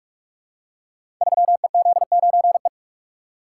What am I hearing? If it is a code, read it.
Morse code, heard as a keyed tone, about 44 wpm, 700 Hz. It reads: 3EZ9E
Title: Code Groups - 5 Letter:Numbers - 45wpm